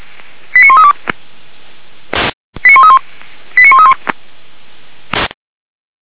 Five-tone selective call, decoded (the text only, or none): 89132 89131 89131